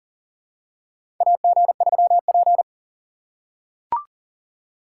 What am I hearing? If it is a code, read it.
Morse code, heard as a keyed tone, about 40 wpm, 700 Hz. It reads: AG3P
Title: USA Call Signs - 4 Characters - 3x Character Spacing - 40wpm